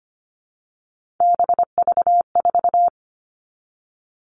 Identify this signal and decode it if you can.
Morse code, heard as a keyed tone, about 25 wpm, 700 Hz. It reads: BV4
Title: All Cut Numbers - 3-Digit 25wpm